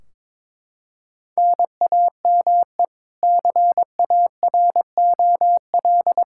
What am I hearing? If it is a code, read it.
Morse code, heard as a keyed tone, about 22 wpm, 700 Hz. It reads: NAME CAROL
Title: Practice QSO - Exchange 1 - Beginning - 22wpm